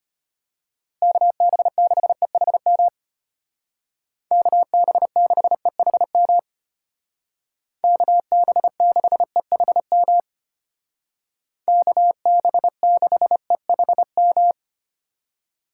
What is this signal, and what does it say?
Morse code, heard as a keyed tone, about 32 wpm, 700 Hz. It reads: KB6EHM KB6EHM KB6EHM KB6EHM